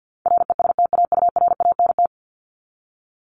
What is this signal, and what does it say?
Morse code, heard as a keyed tone, about 50 wpm, 700 Hz. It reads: RESTAURANT
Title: Top 1751-2000 Words 50wpm